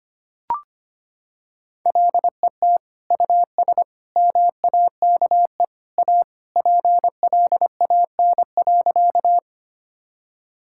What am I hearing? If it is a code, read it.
Morse code, heard as a keyed tone, about 25 wpm, 700 Hz. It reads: LET US MAKE A PLAN.
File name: practice / 25wpm / Sentences with 2-Word Phrases - 25wpm